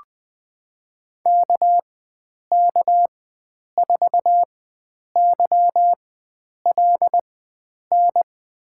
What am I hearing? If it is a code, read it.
Morse code, heard as a keyed tone, about 20 wpm, spaced as for 9 wpm, 700 Hz. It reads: KK4YLN
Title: USA Call Signs - 6 Characters - 4x Character Spacing - 20wpm